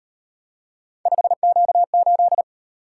Morse code, heard as a keyed tone, about 37 wpm, 700 Hz. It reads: 5Q8